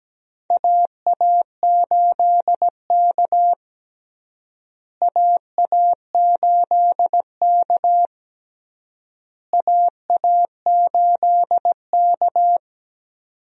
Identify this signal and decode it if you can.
Morse code, heard as a keyed tone, about 17 wpm, 700 Hz. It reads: AA8K AA8K AA8K
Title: USA Call Signs - 4 Characters - Repeated 3x - 17wpm